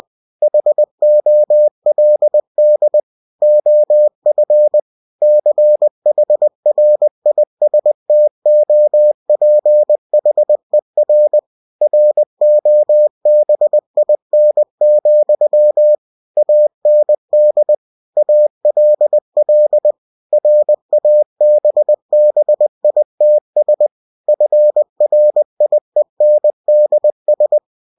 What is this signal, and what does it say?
Morse code, heard as a keyed tone, about 20 wpm, 600 Hz. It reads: HOLD OF CHRISTOPHER ROBIN, AND ALL RABBITS FRIENDS
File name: cw-books / winnie-the-pooh / winnie-20wpm-page-0019